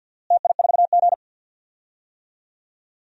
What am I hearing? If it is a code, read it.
Morse code, heard as a keyed tone, about 47 wpm, 700 Hz. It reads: TI4G
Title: DXCC-100 Call Signs - 4 Characters - Repeated 2x - 50wpm